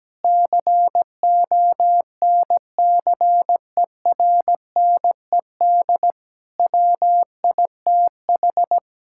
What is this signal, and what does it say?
Morse code, heard as a keyed tone, about 17 wpm, 700 Hz. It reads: CONCERNED WITH
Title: Sentences from Top 1000 Words - Review Part 6 - 17wpm